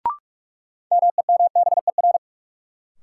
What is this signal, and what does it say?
Morse code, heard as a keyed tone, about 46 wpm, 700 Hz. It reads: MEMBER
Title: Sentences from Top 1100 Words - New Word Drill Alt - 45wpm